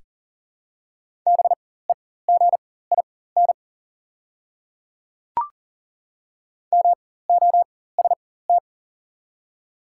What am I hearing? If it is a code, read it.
Morse code, heard as a keyed tone, about 40 wpm, spaced as for 19 wpm, 700 Hz. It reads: BEGIN MOST
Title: 4x Character Spacing - Top 300 Words - 40wpm